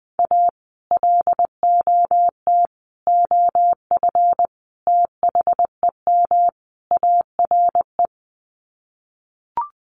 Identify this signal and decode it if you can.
Morse code, heard as a keyed tone, about 20 wpm, 700 Hz. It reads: A LOT OF THEM ARE
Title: Sets of 5 Words Encore2 20wpm